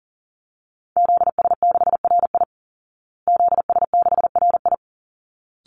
Morse code, heard as a keyed tone, about 40 wpm, 700 Hz. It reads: ZS6RI ZS6RI
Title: Call Signs Repeated 3x - 40wpm